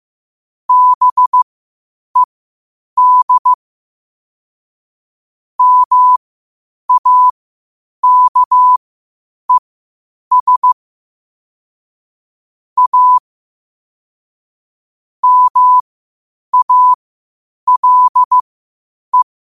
Morse code, heard as a keyed tone, about 15 wpm, spaced as for 8 wpm, 1000 Hz. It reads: BED MAKES A MALE
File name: qotdfiles / 8wpm